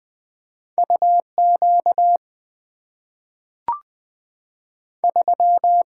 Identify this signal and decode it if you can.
Morse code, heard as a keyed tone, about 20 wpm, 700 Hz. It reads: U Q 3
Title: Two Letter-Number Combinations - 20wpm